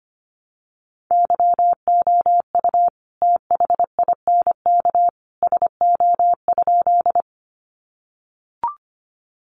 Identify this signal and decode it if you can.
Morse code, heard as a keyed tone, about 25 wpm, 700 Hz. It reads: YOU THINK SO?
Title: Sentences from Top 200 Words 25wpm